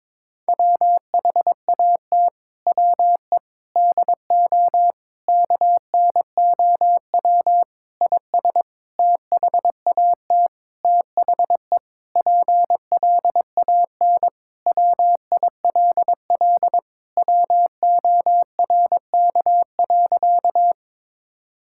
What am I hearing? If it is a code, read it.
Morse code, heard as a keyed tone, about 22 wpm, 700 Hz. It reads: WHAT WE DO KNOW IS THAT THE PLAN WILL WORK.
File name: Sentences with 7-Word Phrases - 22wpm